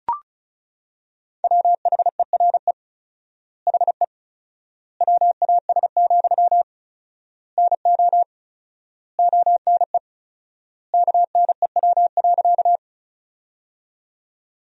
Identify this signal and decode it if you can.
Morse code, heard as a keyed tone, about 34 wpm, 700 Hz. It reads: WHERE HE WAS, NO ONE KNEW.
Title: 4x Word-Spacing - Sentences from Top 400 Words - 35wpm